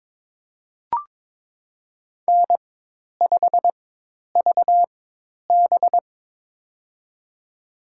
Morse code, heard as a keyed tone, about 22 wpm, spaced as for 10 wpm, 700 Hz. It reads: N5VB
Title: USA Call Signs - 4 Characters - 4x Character Spacing - 22wpm